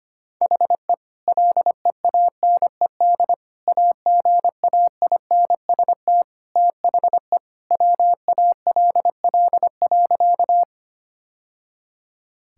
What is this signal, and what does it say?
Morse code, heard as a keyed tone, about 25 wpm, 700 Hz. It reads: HE LEANED AGAINST THE WALL.
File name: practice / 25wpm / Sentences with 2-Word Phrases - 25wpm